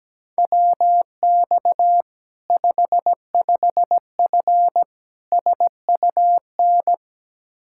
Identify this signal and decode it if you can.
Morse code, heard as a keyed tone, about 17 wpm, 700 Hz. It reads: WX 55F SUN
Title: K1USN SST QSO Practice 17wpm